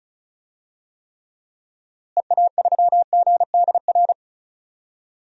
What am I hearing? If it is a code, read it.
Morse code, heard as a keyed tone, about 36 wpm, 700 Hz. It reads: EA3GDR
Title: DXCC-100 Call Signs - 6 Characters - Repeated 2x - 35wpm